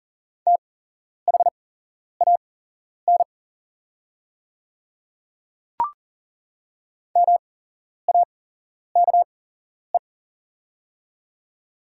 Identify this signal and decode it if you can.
Morse code, heard as a keyed tone, about 40 wpm, spaced as for 11 wpm, 700 Hz. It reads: THAN MAKE
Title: 8x Character Spacing - Top 100 Words - 40wpm